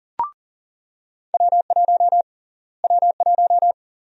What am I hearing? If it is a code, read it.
Morse code, heard as a keyed tone, about 40 wpm, 700 Hz. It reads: W1 W1
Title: USA Call Sign Prefixes - Repeated 2x - 40wpm